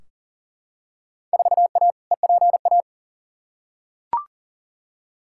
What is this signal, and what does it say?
Morse code, heard as a keyed tone, about 40 wpm, 700 Hz. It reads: 4A EPA
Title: Field Day Exchanges - 40wpm